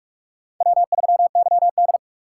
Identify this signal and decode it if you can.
Morse code, heard as a keyed tone, about 44 wpm, 690 Hz. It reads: W3YD